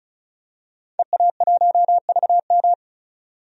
Morse code, heard as a keyed tone, about 34 wpm, 700 Hz. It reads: EA1VM
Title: DXCC-100 Call Signs - 5 Characters - 35wpm